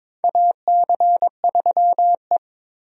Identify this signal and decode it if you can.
Morse code, heard as a keyed tone, about 21 wpm, 700 Hz. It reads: AC3E
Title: USA Call Signs - 4 Characters - 22wpm